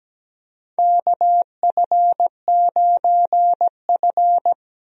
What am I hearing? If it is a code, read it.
Morse code, heard as a keyed tone, about 17 wpm, 700 Hz. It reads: KF9F
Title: USA Call Signs - 4 Characters - Repeated 2x - 17wpm